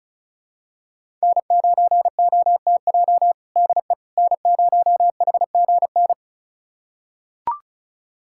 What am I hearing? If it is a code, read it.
Morse code, heard as a keyed tone, about 34 wpm, 700 Hz. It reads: N9OTJ DE N0HGN
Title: USA Callsign DE Callsign - 5 Characters - 35wpm